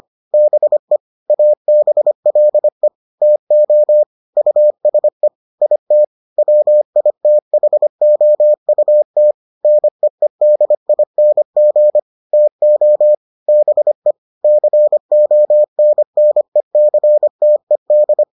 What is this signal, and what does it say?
Morse code, heard as a keyed tone, about 25 wpm, 600 Hz. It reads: BE ABLE TO USE IT WITHOUT NEEDING TO BE CONNECTED